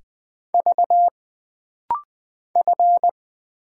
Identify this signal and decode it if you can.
Morse code, heard as a keyed tone, about 20 wpm, 700 Hz. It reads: V F
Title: Single Letters - Rapid-Fire 20wpm